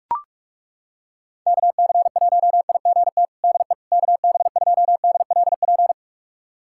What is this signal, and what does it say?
Morse code, heard as a keyed tone, about 41 wpm, 700 Hz. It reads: KC1IGT DE KB2DLP
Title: USA Callsign DE Callsign - 6 Characters - 45wpm